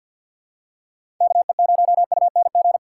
Morse code, heard as a keyed tone, about 53 wpm, 700 Hz. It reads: KE0UNG
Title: USA Call Signs - 6 Characters - 4x Character Spacing - 50wpm